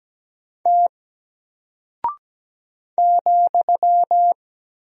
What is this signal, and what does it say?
Morse code, heard as a keyed tone, about 17 wpm, 700 Hz. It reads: T ,
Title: Single Character (Letter:Number:Punctuation:Prosign) 17wpm